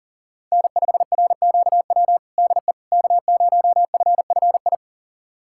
Morse code, heard as a keyed tone, about 40 wpm, 700 Hz. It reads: N5RQW DE K0FFI
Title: USA Callsign DE Callsign 40wpm